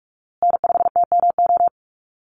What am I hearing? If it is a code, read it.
Morse code, heard as a keyed tone, about 44 wpm, 700 Hz. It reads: N5TMO
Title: USA Call Signs Repeated 3x 45wpm